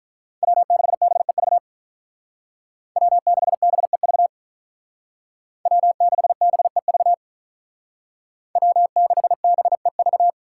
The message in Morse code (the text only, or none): W6BEV W6BEV W6BEV W6BEV